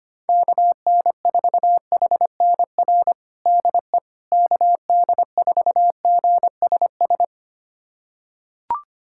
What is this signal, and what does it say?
Morse code, heard as a keyed tone, about 25 wpm, 700 Hz. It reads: KN4HNR DE KD4GSS
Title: USA Callsign DE Callsign - 6 Characters - 25wpm